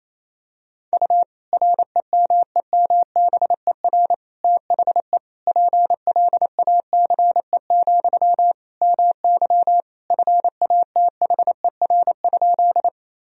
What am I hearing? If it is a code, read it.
Morse code, heard as a keyed tone, about 27 wpm, 700 Hz. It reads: U REMEMBER THE PLACE, MY FATHER?
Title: Sentences from Top 400 Words and Abbreviations 28wpm